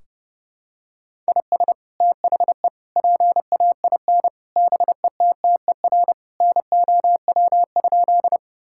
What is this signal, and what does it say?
Morse code, heard as a keyed tone, about 30 wpm, 700 Hz. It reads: IS THE PAIN BETTER NOW?
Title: Sentences from Top 1600 Words 30wpm